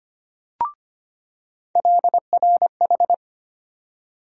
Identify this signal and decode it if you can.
Morse code, heard as a keyed tone, about 25 wpm, 700 Hz. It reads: LRH
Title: USA Call Sign Postfixes - 3 Letters 25wpm